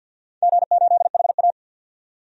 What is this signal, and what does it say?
Morse code, heard as a keyed tone, about 51 wpm, 700 Hz. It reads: G8HA
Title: DXCC-100 Call Signs - 4 Characters - 50wpm